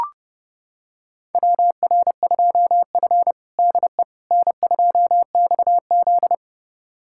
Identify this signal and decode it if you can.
Morse code, heard as a keyed tone, about 30 wpm, 700 Hz. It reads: WR2F DE N2XZ
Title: USA Callsign DE Callsign - 4 Characters - 30wpm